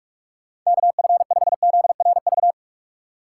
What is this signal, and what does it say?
Morse code, heard as a keyed tone, about 46 wpm, 700 Hz. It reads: KF5ZRV
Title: USA Call Signs - 3x Character Spacing - 45wpm